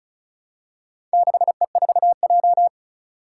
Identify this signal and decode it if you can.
Morse code, heard as a keyed tone, about 36 wpm, 700 Hz. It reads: 6E4J
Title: Four Challenging Letter-Number Combinations 35wpm